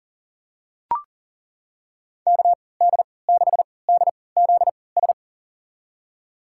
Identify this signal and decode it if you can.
Morse code, heard as a keyed tone, about 40 wpm, spaced as for 23 wpm, 700 Hz. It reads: KD6DZS
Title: USA Call Signs - 6 Characters - 3x Character Spacing - 40wpm